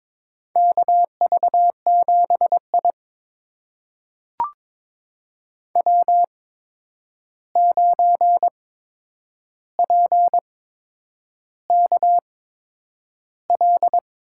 Morse code, heard as a keyed tone, about 22 wpm, spaced as for 9 wpm, 700 Hz. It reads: KV7I W9PKL
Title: USA Call Signs - 8x Character Spacing - 22wpm